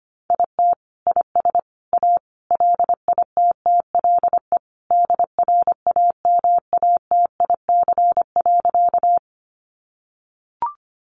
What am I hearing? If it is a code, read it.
Morse code, heard as a keyed tone, about 25 wpm, 700 Hz. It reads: IT IS A LITTLE DRAMATIC.
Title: Sentences from Top 1300 Words 25wpm